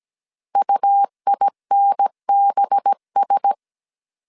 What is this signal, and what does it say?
Morse code, heard as a keyed tone, about 17 wpm, 800 Hz. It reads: UINBS
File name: CW / DET / 1 6 1500C S A 16wpm 2635